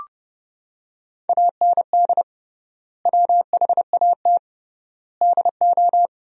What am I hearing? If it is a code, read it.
Morse code, heard as a keyed tone, about 32 wpm, 700 Hz. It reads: AND WHAT DO
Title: 3x Word-Spacing - Sentences from Top 100 Words - 30wpm